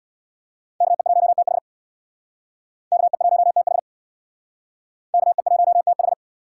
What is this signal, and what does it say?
Morse code, heard as a keyed tone, about 64 wpm, 700 Hz. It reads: KE1TH KE1TH KE1TH